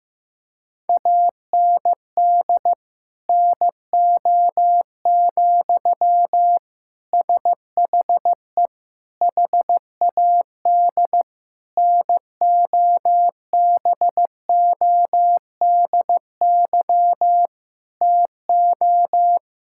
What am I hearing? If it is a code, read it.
Morse code, heard as a keyed tone, about 16 wpm, 700 Hz. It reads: AND NO, SHE HAD NOBODY TO
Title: Sentences from Top 1600 Words 15wpm